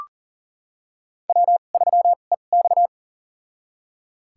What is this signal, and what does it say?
Morse code, heard as a keyed tone, about 40 wpm, spaced as for 29 wpm, 700 Hz. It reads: W3EX
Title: USA Call Signs - 4 Characters - 2x Character Spacing - 40wpm